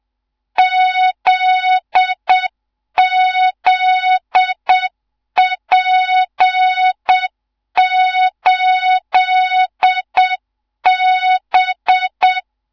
Morse code, heard as a keyed tone, about 7 wpm, 3700 Hz, with noise. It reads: ZZP8B
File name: Morze-HA2MN / hang / 15 GYAKORLAT